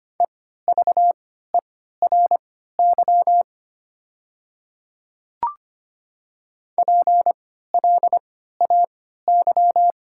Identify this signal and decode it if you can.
Morse code, heard as a keyed tone, about 25 wpm, spaced as for 14 wpm, 700 Hz. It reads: EVERY PLAY
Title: 3x Character Spacing - Top 200 Words - 25wpm